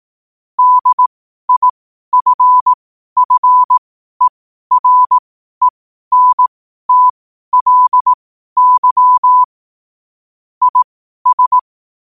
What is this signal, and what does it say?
Morse code, heard as a keyed tone, about 18 wpm, spaced as for 12 wpm, 1000 Hz. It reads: DIFFERENTLY IS
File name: qotdfiles / 12wpm